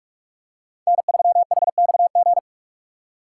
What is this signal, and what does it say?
Morse code, heard as a keyed tone, about 43 wpm, 700 Hz. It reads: N3HXG
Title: USA Call Signs - 5 Characters - 3x Character Spacing - 45wpm